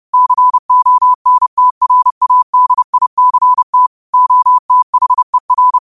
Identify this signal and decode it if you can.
Morse code, heard as a keyed tone, about 30 wpm, 1000 Hz. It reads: CONTRADICT OTHER
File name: qotdfiles / 30wpm